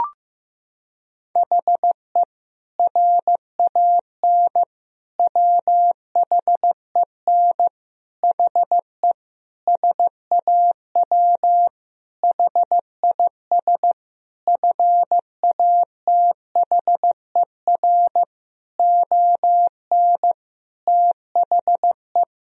Morse code, heard as a keyed tone, about 15 wpm, 700 Hz. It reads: HE RAN WHEN HE SAW HIS FATHER ON THE
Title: Sentences from Top 500 Words and Abbreviations 15wpm